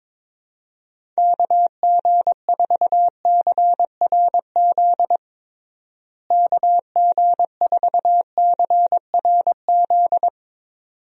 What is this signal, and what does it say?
Morse code, heard as a keyed tone, about 22 wpm, 700 Hz. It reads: KG4CRZ KG4CRZ
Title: USA Call Signs - 6 Characters - Repeated 2x - 22wpm